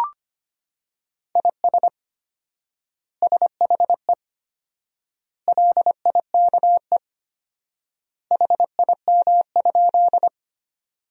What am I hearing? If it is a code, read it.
Morse code, heard as a keyed tone, about 25 wpm, 700 Hz. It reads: IS SHE LIKE HIM?
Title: 4x Word-Spacing - Sentences from Top 100 Words - 25wpm